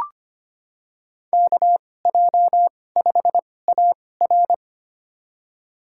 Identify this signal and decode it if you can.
Morse code, heard as a keyed tone, about 25 wpm, spaced as for 18 wpm, 700 Hz. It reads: KJ5AR
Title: USA Call Signs - 5 Characters - 2x Character Spacing - 25wpm